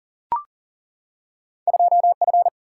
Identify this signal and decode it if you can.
Morse code, heard as a keyed tone, about 40 wpm, 700 Hz. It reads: 2F
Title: Single-Letter Single-Number 40wpm